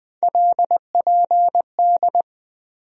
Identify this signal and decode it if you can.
Morse code, heard as a keyed tone, about 20 wpm, 700 Hz. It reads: LPD
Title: USA Call Sign Postfixes - 3 Letters 20wpm